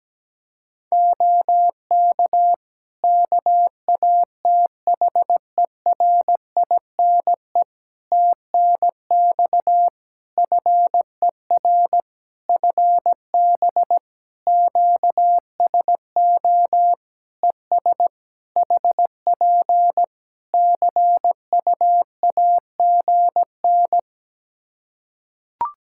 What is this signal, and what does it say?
Morse code, heard as a keyed tone, about 17 wpm, 700 Hz. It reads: OK KATHERINE TNX FER FB QSO ES HP CUAGN
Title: K1USN SST QSO Practice 17wpm